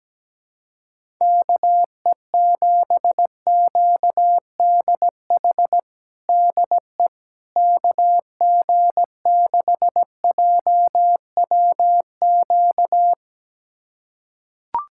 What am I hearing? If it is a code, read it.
Morse code, heard as a keyed tone, about 17 wpm, 700 Hz. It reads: KE7QDH DE KG6JWQ
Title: USA Callsign DE Callsign - 6 Characters - 17wpm